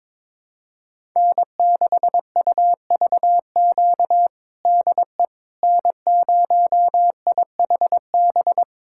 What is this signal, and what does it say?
Morse code, heard as a keyed tone, about 22 wpm, 700 Hz. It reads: N6UVQ DE N0IHB
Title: USA Callsign DE Callsign - 5 Characters - 22wpm